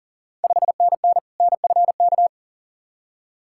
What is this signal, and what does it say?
Morse code, heard as a keyed tone, about 40 wpm, 700 Hz. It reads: 5NN NFK
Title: Ontario QSO Party Exchange 40wpm